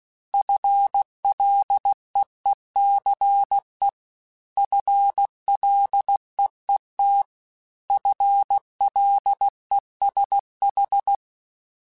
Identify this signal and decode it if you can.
Morse code, heard as a keyed tone, about 16 wpm, 800 Hz. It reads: FLEECE FLEET FLESH
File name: FL Runs / FL Runs mp3s / FL Runs @15